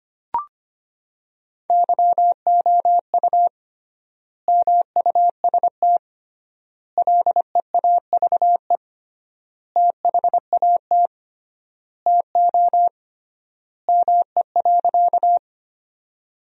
Morse code, heard as a keyed tone, about 25 wpm, 700 Hz. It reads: YOU MUST LEAVE THAT TO ME.
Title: 3x Word-Spacing - Sentences from Top 400 Words - 25wpm